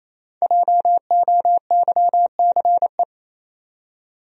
Morse code, heard as a keyed tone, about 28 wpm, 700 Hz. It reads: JOYCE
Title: USA Common Names - Max 5 Letters - 28wpm